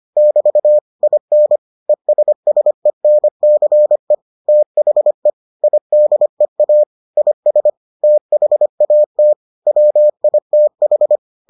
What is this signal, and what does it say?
Morse code, heard as a keyed tone, about 25 wpm, 600 Hz. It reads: <BT> IN ESSENCE THE IDEA IS THAT WITH